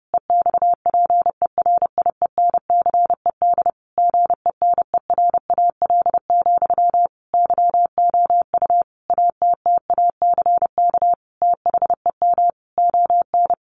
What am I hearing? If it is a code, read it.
Morse code, heard as a keyed tone, about 30 wpm, 700 Hz. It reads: EXPERIENCED GENERAL, YOU ATTACK THEM ON
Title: Sentences from Top 1500 Words 30wpm